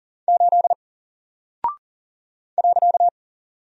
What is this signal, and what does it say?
Morse code, heard as a keyed tone, about 40 wpm, 700 Hz. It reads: 8 .
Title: Single Character (Letter:Number:Punctuation:Prosign) 40wpm